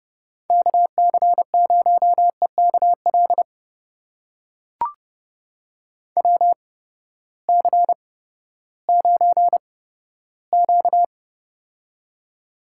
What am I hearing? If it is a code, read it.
Morse code, heard as a keyed tone, about 30 wpm, spaced as for 15 wpm, 700 Hz. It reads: KC0EKL WC9Q
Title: USA Call Signs - 8x Character Spacing - 30wpm